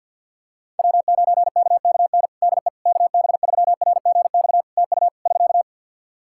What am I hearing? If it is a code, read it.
Morse code, heard as a keyed tone, about 53 wpm, 700 Hz. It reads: W9KKN DE KB3RGX TU <SK>
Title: Practice QSO - Exchange 4 - Ending - 50wpm